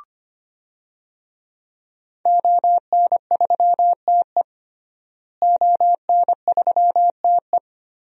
Morse code, heard as a keyed tone, about 25 wpm, 700 Hz. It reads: ON3TE ON3TE
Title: DXCC-100 Call Signs - 5 Characters - Repeated 2x - 25wpm